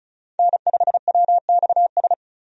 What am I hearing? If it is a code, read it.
Morse code, heard as a keyed tone, about 36 wpm, 700 Hz. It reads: N5WXS
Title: USA Call Signs - 5 Characters - 35wpm